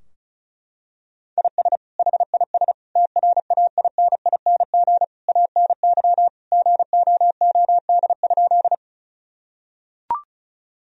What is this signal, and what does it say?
Morse code, heard as a keyed tone, about 36 wpm, 700 Hz. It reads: IS HIS TRAINING ANY GOOD?
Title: Sentences from Top 1000 Words - Review Part 4 - 35wpm